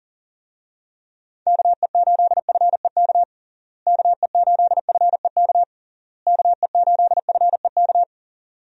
Morse code, heard as a keyed tone, about 37 wpm, 700 Hz. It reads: KE8FEK KE8FEK KE8FEK
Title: USA Call Signs - 6 Characters - Repeated 3x - 40wpm